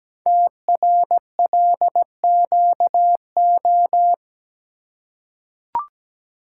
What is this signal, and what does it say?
Morse code, heard as a keyed tone, about 17 wpm, 700 Hz. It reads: TRLQO
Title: Code Groups - 5 Letter:Numbers - 17wpm